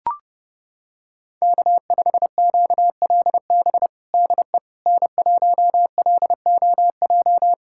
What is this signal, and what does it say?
Morse code, heard as a keyed tone, about 30 wpm, 700 Hz. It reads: K5QLB DE N1LOJ
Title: USA Callsign DE Callsign - 5 Characters - 30wpm